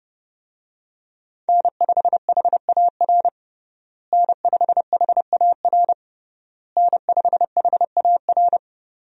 Morse code, heard as a keyed tone, about 32 wpm, 700 Hz. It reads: N5HAR N5HAR N5HAR
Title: USA Call Signs - 5 Characters - Repeated 3x - 30wpm